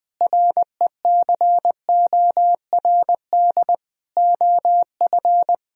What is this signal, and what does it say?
Morse code, heard as a keyed tone, about 20 wpm, 700 Hz. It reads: RECORD OF
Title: Sentences from Top 1000 Words - Review Part 3 - 20wpm